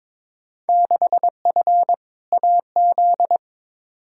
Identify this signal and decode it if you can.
Morse code, heard as a keyed tone, about 21 wpm, 700 Hz. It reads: 6F AZ
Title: Field Day Exchanges Encore - 22wpm